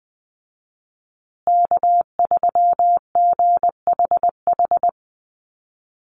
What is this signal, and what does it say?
Morse code, heard as a keyed tone, about 20 wpm, 700 Hz. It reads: K3GHH
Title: Call Signs Repeated 3x - 20wpm